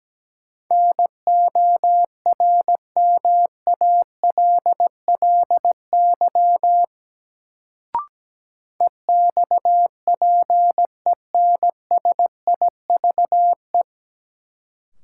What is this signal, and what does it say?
Morse code, heard as a keyed tone, about 17 wpm, 700 Hz. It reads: NORMALLY EXPENSIVE
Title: Sentences from Top 1000 Words - Review Part 6 - New Word Drill - 17wpm